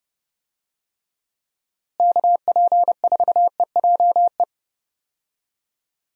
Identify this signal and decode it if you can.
Morse code, heard as a keyed tone, about 32 wpm, 700 Hz. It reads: KP4EJE
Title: DXCC-100 Call Signs - 6 Characters - Repeated 3x - 30wpm